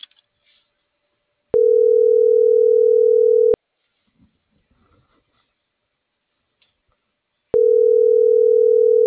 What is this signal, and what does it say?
Ringback tone (North America), 440 + 480 Hz.